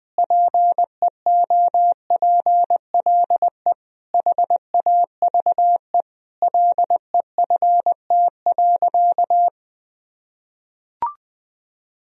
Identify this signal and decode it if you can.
Morse code, heard as a keyed tone, about 20 wpm, 700 Hz. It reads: PEOPLE HAVE LEFT.
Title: Sentences from Top 1000 Words - Review Part 1 - 20wpm